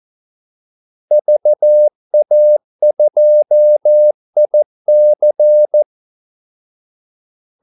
Morse code, heard as a keyed tone, about 14 wpm, 600 Hz. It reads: VA2IC
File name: Callsigns-14wpm